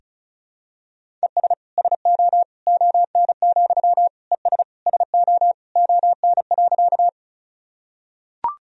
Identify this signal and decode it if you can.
Morse code, heard as a keyed tone, about 34 wpm, 700 Hz. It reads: ES SO ON, ES SO ON.
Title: Sentences from Top 100 Words and Abbreviations 35wpm